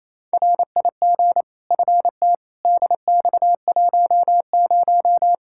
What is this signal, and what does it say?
Morse code, heard as a keyed tone, about 27 wpm, 700 Hz. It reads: RIG FT DX10
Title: Practice QSO - Exchange 2 - Info - 28wpm